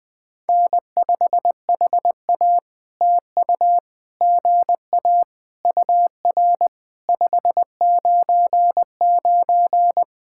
Morse code, heard as a keyed tone, about 20 wpm, 700 Hz. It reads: N5HA TU GA UR 599
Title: POTA QSO Practice - 20wpm